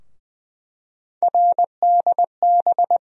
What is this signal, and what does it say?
Morse code, heard as a keyed tone, about 20 wpm, 700 Hz. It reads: RDB
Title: USA Call Sign Postfixes - 3 Letters 20wpm